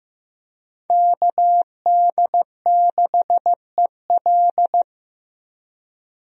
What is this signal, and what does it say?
Morse code, heard as a keyed tone, about 15 wpm, 700 Hz. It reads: KD6EL
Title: USA Call Signs - 5 Characters - Repeated 2x - 15wpm